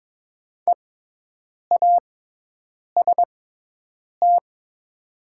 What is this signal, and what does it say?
Morse code, heard as a keyed tone, about 22 wpm, spaced as for 8 wpm, 700 Hz. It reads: EAST
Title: 6x Character Spacing - Top 500 Words - 22wpm